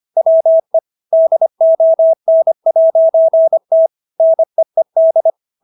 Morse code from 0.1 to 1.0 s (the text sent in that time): WE